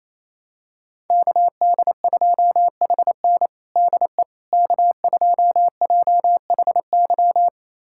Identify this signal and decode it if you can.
Morse code, heard as a keyed tone, about 28 wpm, 700 Hz. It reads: KD2HN DE K2JHY
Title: USA Callsign DE Callsign - 5 Characters - 28wpm